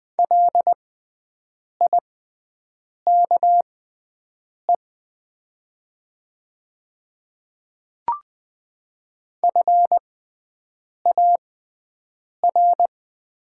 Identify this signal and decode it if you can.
Morse code, heard as a keyed tone, about 20 wpm, spaced as for 7 wpm, 700 Hz. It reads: LIKE FAR
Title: 6x Character Spacing - Top 300 Words - 20wpm